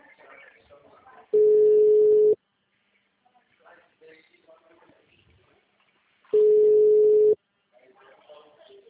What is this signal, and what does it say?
Ringback tone (Europe, 425 Hz), with speech.
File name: out- -2012- - - .42